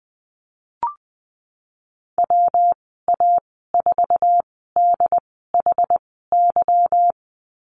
Morse code, heard as a keyed tone, about 20 wpm, spaced as for 14 wpm, 700 Hz. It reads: WA4DHY
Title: 167 Lesson 27g - Number 7 - Call Signs 20wpm